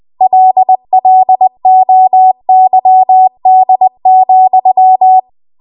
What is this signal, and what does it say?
Morse code, heard as a keyed tone, about 20 wpm, 750 Hz. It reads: LLOYD,
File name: Morse / 20WPM